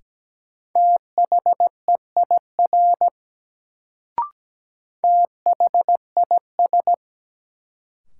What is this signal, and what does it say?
Morse code, heard as a keyed tone, about 17 wpm, 700 Hz. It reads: THEIR THIS
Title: Sets of 2 Words - New Word Drill - 17wpm